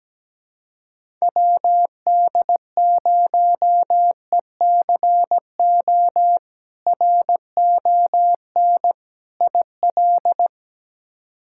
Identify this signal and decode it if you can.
Morse code, heard as a keyed tone, about 17 wpm, 700 Hz. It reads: WD0ECO RON IL
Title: K1USN SST QSO Practice - 17wpm